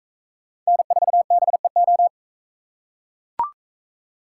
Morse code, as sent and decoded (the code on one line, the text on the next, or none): -. ....- -... ---
N4BEO